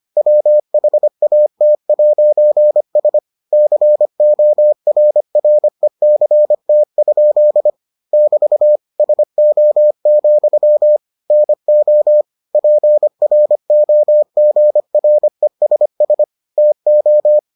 Morse code, heard as a keyed tone, about 25 wpm, 600 Hz. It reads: WHAT'S CORRECT? <BT> SO, NO PROGRESS TO